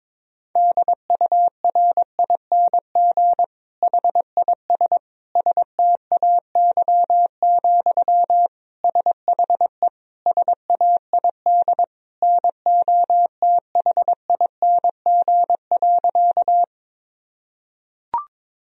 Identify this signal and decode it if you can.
Morse code, heard as a keyed tone, about 22 wpm, 700 Hz. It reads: DURING HIS STAY, SHE SAID NOTHING.
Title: Sentences from Top 1000 Words - Review Part 2 - 22wpm